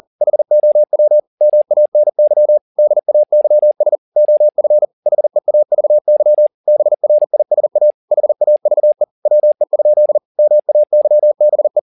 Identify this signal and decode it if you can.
Morse code, heard as a keyed tone, about 40 wpm, 600 Hz. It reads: HOW MANY DAYS OF HEAVY BRISA HAVE WE? MAYBE
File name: the old man and the sea 2 40wpm